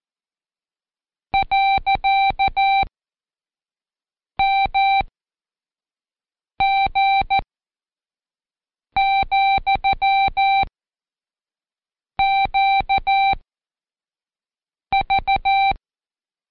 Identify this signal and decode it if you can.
Morse code, heard as a keyed tone, about 14 wpm, 780 Hz. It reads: . M G , Q V
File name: Code Quick / AudioDownld1 / Seg65-Day7-Pract Comma